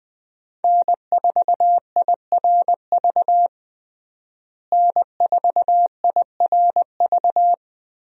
Morse code, heard as a keyed tone, about 20 wpm, 700 Hz. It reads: N4IRV N4IRV
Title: USA Call Signs - 5 Characters - Repeated 2x - 20wpm